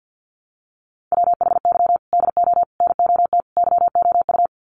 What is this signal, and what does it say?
Morse code, heard as a keyed tone, about 52 wpm, 700 Hz. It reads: WHY DO NOT YOU
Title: Sets of 4 Words 50wpm